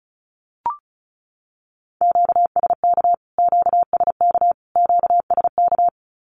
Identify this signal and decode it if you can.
Morse code, heard as a keyed tone, about 34 wpm, 700 Hz. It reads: QSK QSK QSK
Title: Q-Codes Repeated 3x 35wpm